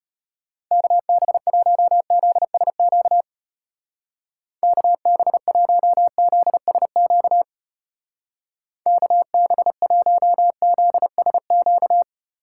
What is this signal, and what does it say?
Morse code, heard as a keyed tone, about 32 wpm, 700 Hz. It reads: KB1ZSQ KB1ZSQ KB1ZSQ